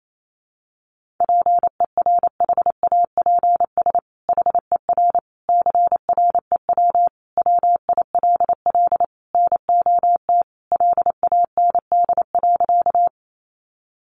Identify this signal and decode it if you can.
Morse code, heard as a keyed tone, about 28 wpm, 700 Hz. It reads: PERHAPS HER CREW WILL NOT LAND.
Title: Sentences from Top 1500 Words 28wpm